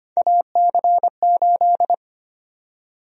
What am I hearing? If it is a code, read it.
Morse code, heard as a keyed tone, about 25 wpm, 700 Hz. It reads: AC8
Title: USA Call Sign Prefixes - Repeated 2x - 25wpm